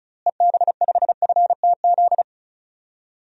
Morse code, heard as a keyed tone, about 34 wpm, 700 Hz. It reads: EB5FTZ
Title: DXCC-100 Call Signs - 6 Characters - Repeated 2x - 35wpm